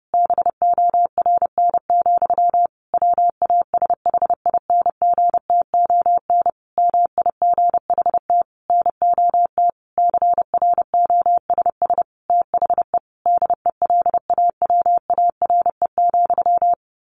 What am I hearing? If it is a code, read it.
Morse code, heard as a keyed tone, about 30 wpm, 700 Hz. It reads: BORN, WASHINGTON MIGHT NOT CROSS THE DELAWARE,
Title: A Sound Of Thunder 30wpm